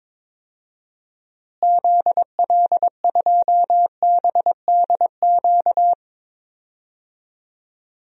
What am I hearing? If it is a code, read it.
Morse code, heard as a keyed tone, about 22 wpm, 700 Hz. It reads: ZL2BDQ
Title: DXCC-100 Call Signs - 6 Characters - Repeated 2x - 22wpm